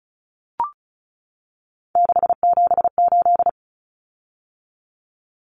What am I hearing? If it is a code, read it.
Morse code, heard as a keyed tone, about 34 wpm, 700 Hz. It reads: 678
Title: Common Cut Numbers - 3-Digit 35wpm